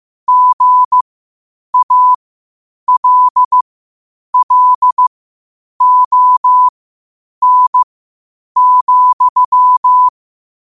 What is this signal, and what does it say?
Morse code, heard as a keyed tone, about 15 wpm, spaced as for 8 wpm, 1000 Hz. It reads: GALLON,
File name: qotdfiles / 8wpm